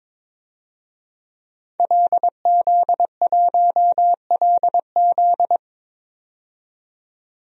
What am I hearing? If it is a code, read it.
Morse code, heard as a keyed tone, about 21 wpm, 700 Hz. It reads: LZ1LZ
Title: DXCC-100 Call Signs - 5 Characters - 22wpm